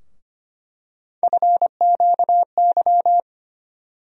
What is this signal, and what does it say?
Morse code, heard as a keyed tone, about 25 wpm, 700 Hz. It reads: FQY
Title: USA Call Sign Postfixes - 3 Letters 25wpm